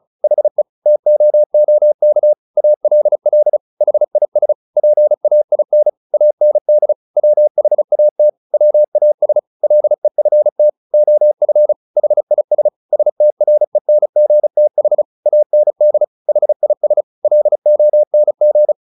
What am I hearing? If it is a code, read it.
Morse code, heard as a keyed tone, about 34 wpm, 600 Hz. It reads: HE TOOK ALL HIS PAIN AND WHAT WAS LEFT OF HIS STRENGTH AND HIS LONG
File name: the old man and the sea 2 35wpm